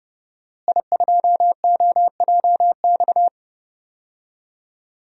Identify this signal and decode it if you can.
Morse code, heard as a keyed tone, about 32 wpm, 700 Hz. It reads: I2OJX
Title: DXCC-100 Call Signs - 5 Characters - Repeated 3x - 30wpm